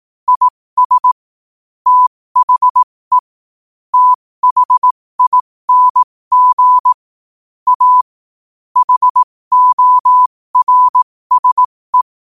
Morse code, heard as a keyed tone, about 18 wpm, spaced as for 15 wpm, 1000 Hz. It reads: IS THE THING A HORSE
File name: qotdfiles / 15wpm